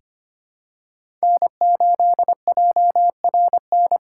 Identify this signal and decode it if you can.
Morse code, heard as a keyed tone, about 25 wpm, 700 Hz. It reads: N8JRN
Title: USA Call Signs - 5 Characters - 4x Character Spacing - 25wpm